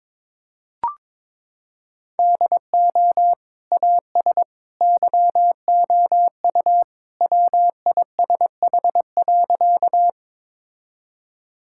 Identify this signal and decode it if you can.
Morse code, heard as a keyed tone, about 22 wpm, 700 Hz. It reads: DO AS YOU WISH.
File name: practice / 22wpm / Sentences with 2-Word Phrases - 22wpm